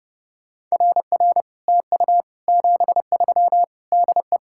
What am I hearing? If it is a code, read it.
Morse code, heard as a keyed tone, about 38 wpm, 700 Hz. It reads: RR TU 73 DE